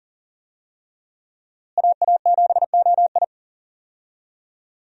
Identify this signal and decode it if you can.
Morse code, heard as a keyed tone, about 40 wpm, 700 Hz. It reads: AA7OI
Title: USA Call Signs - 5 Characters - Repeated 3x - 40wpm